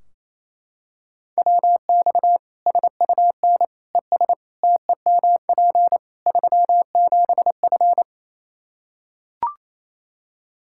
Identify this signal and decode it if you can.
Morse code, heard as a keyed tone, about 28 wpm, 700 Hz. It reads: WX SUN ES TEMP 37F
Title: K1USN SST QSO Practice 28wpm